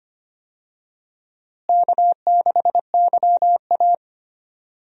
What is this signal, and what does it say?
Morse code, heard as a keyed tone, about 25 wpm, 700 Hz. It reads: K6YA
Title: USA Call Signs - 4 Characters - Repeated 2x - 25wpm